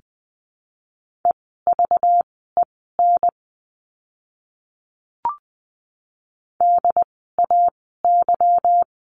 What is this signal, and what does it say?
Morse code, heard as a keyed tone, about 20 wpm, spaced as for 15 wpm, 700 Hz. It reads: EVEN DAY
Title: 237 Lesson 38f - Letter q - Words 20wpm